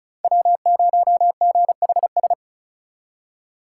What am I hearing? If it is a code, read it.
Morse code, heard as a keyed tone, about 46 wpm, 700 Hz. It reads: W0GHS